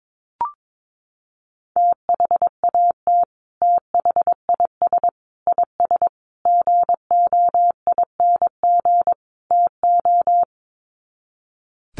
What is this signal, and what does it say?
Morse code, heard as a keyed tone, about 22 wpm, 700 Hz. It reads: THAT THIS IS GOING TO
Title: Sets of 5 Words Encore2 22wpm